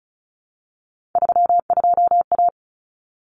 Morse code, heard as a keyed tone, about 34 wpm, 700 Hz. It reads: 32A
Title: Common Cut Numbers - 3-Digit 35wpm